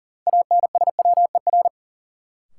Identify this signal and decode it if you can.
Morse code, heard as a keyed tone, about 40 wpm, 700 Hz. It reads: ANSWER
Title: Sentences from Top 300 Words - New Word Drill - 40wpm